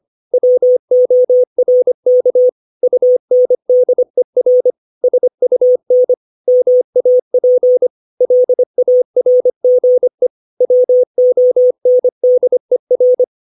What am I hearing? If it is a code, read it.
Morse code, heard as a keyed tone, about 25 wpm, 500 Hz. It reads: WORK UNDER SUN MAP LARGE WONDER
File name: 632 words / 632 Words 25WPM